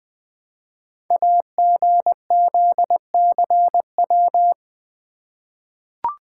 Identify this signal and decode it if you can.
Morse code, heard as a keyed tone, about 20 wpm, 700 Hz. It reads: AGZCW
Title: Code Groups - 5 Letters - 20wpm